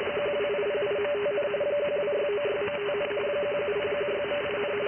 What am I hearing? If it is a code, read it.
RTTY, about 45.45 baud. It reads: 12808.5, 12696.5, 17016.8,
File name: IC-7100